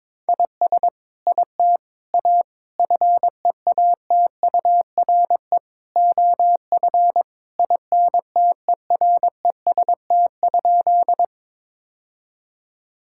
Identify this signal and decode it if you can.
Morse code, heard as a keyed tone, about 22 wpm, 700 Hz. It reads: IS IT A FEATURE OF INTEREST?
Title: Sentences from Top 1000 Words - Review Part 5 - 22wpm